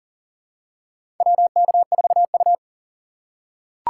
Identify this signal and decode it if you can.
Morse code, heard as a keyed tone, about 40 wpm, 700 Hz. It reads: WK4U
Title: USA Call Signs - 4 Characters - 3x Character Spacing - 40wpm